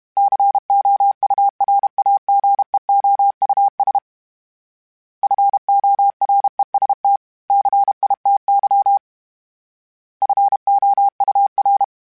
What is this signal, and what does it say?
Morse code, heard as a keyed tone, about 32 wpm, 800 Hz. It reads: COURAGEOUS FOREST CITY FOUR
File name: fromSMs / FromSMs mp3s / Cities@30